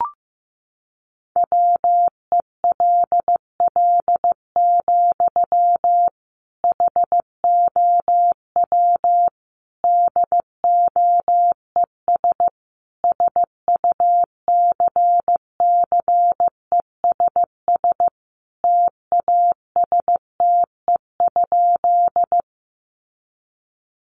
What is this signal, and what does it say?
Morse code, heard as a keyed tone, about 15 wpm, 700 Hz. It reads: WELL, HOW DOES SUCCESS TASTE?
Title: Sentences from Top 1100 Words 15wpm